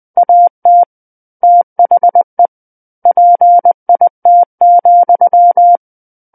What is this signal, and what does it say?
Morse code, heard as a keyed tone, about 20 wpm, 700 Hz. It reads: AT THE PIT,
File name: 20 WPM